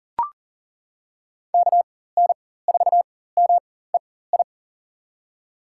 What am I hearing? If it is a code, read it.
Morse code, heard as a keyed tone, about 40 wpm, spaced as for 19 wpm, 700 Hz. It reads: KN4MEI
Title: USA Call Signs - 6 Characters - 4x Character Spacing - 40wpm